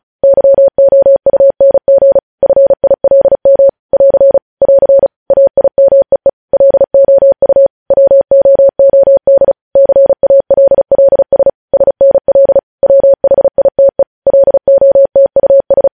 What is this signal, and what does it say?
Morse code, heard as a keyed tone, about 36 wpm, 550 Hz. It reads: YOUNG FILM <AR> <AR> AIMEE LOU WOOD CALLS SNL WHITE LOTUS
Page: news headlines in morse code at 35 wpm